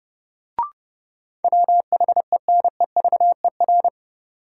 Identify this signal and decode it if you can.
Morse code, heard as a keyed tone, about 30 wpm, 700 Hz. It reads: WHENEVER
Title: Sentences from Top 1200 Words - New Word Drill Alt - 30wpm